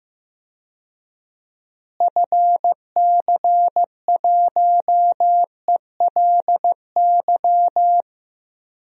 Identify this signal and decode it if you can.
Morse code, heard as a keyed tone, about 15 wpm, 700 Hz. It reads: FC1ELY
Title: DXCC-100 Call Signs - 6 Characters - Repeated 2x - 15wpm